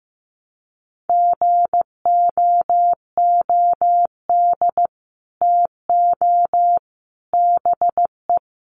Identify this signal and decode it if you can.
Morse code, heard as a keyed tone, about 14 wpm, 700 Hz. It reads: GOOD TO BE
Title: Sets of 5 Words Encore2 15wpm